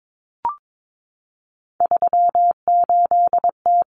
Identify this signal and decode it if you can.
Morse code, heard as a keyed tone, about 22 wpm, 700 Hz. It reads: 38T
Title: Common Cut Numbers - 3-Digit 22wpm